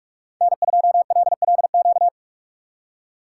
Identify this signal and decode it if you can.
Morse code, heard as a keyed tone, about 44 wpm, 700 Hz. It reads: N2LLQ